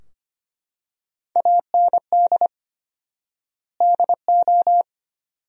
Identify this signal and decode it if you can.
Morse code, heard as a keyed tone, about 25 wpm, 700 Hz. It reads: AND DO
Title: 4x Word-Spacing - Sentences from Top 100 Words - 25wpm